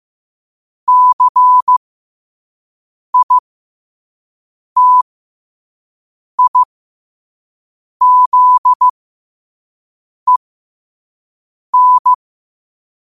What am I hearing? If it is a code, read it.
Morse code, heard as a keyed tone, about 15 wpm, spaced as for 5 wpm, 1000 Hz. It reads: CITIZEN
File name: qotdfiles / 5wpm